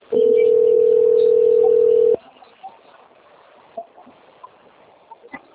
Ringback tone (North America), 440 + 480 Hz.